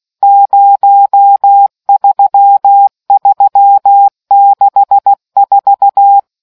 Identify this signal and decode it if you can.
Morse code, heard as a keyed tone, about 16 wpm, 800 Hz. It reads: 03364